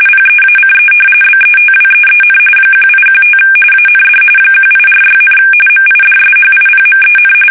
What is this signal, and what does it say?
RTTY, about 45.45 baud. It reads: PEACE BETWEEN 1815 AND 1846, MIDSHIPMEN
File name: fsk850 / usnavy2